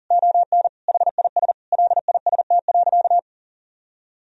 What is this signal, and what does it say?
Morse code, heard as a keyed tone, about 40 wpm, 700 Hz. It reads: ON HIS LIST.
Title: Sentences from Top 1000 Words - Review Part 2 - 40wpm